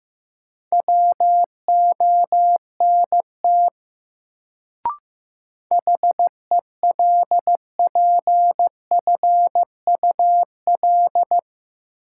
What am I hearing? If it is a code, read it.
Morse code, heard as a keyed tone, about 15 wpm, 700 Hz. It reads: WONT HELPFUL
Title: Sentences from Top 1000 Words - Review Part 6 - New Word Drill - 15wpm